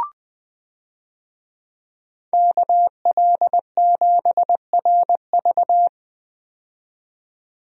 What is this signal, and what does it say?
Morse code, heard as a keyed tone, about 20 wpm, 700 Hz. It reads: KL7RV
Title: USA Call Signs - 5 Characters - 20wpm